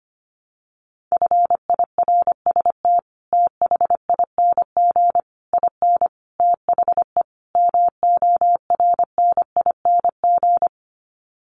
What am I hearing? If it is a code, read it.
Morse code, heard as a keyed tone, about 25 wpm, 700 Hz. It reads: FIRST THING IN THE MORNING
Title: Sets of 5 Words Encore2 25wpm